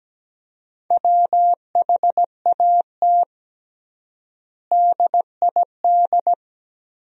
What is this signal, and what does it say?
Morse code, heard as a keyed tone, about 17 wpm, 700 Hz. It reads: WHAT DID
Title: 3x Word-Spacing - Sentences from Top 300 Words - 17wpm